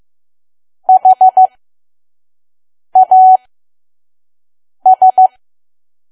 Morse code, heard as a keyed tone, about 15 wpm, spaced as for 5 wpm, 750 Hz. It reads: HAS
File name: Archive / 5 WPM files / PM